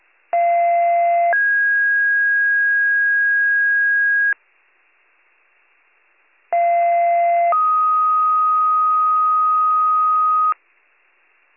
Two-tone page, heard as a one-second tone, then a three-second tone, about 690 then 1650 Hz, about 690 then 1250 Hz, with noise. It is heard from 0.3 to 10.5 s.